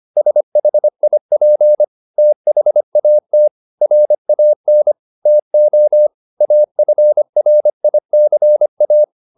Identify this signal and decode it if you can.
Morse code, heard as a keyed tone, about 25 wpm, 600 Hz. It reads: SHIP THAT RAN TO AFRICA
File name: the old man and the sea part 1 25wpm